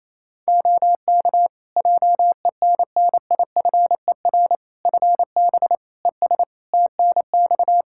Morse code, heard as a keyed tone, about 28 wpm, 700 Hz. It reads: OK JENNIFER FB ES TNX
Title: Practice QSO - Exchange 2 - Info - 28wpm